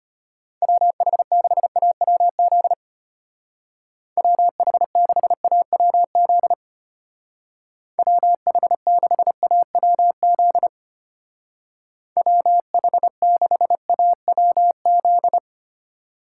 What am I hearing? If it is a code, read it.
Morse code, heard as a keyed tone, about 31 wpm, 700 Hz. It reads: WH6AWZ WH6AWZ WH6AWZ WH6AWZ
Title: Speed Racing - USA 2x3 Call Signs 25wpm